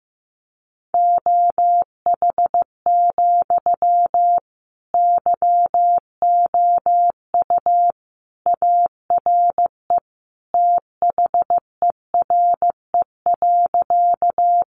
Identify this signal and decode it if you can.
Morse code, heard as a keyed tone, about 15 wpm, 700 Hz. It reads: OH, YOU ARE THERE.
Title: Sentences from Top 500 Words 15wpm